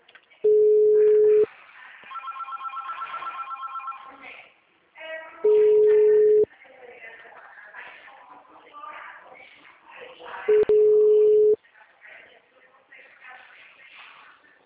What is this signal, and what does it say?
Ringback tone (Europe, 425 Hz), with speech.